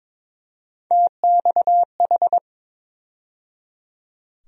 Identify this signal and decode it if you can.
Morse code, heard as a keyed tone, about 22 wpm, 700 Hz. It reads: TXH
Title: USA Call Sign Postfixes - 3 Letters 22wpm